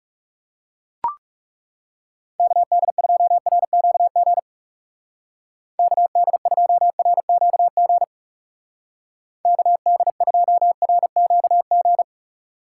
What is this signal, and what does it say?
Morse code, heard as a keyed tone, about 40 wpm, 700 Hz. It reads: KD2RQG KD2RQG KD2RQG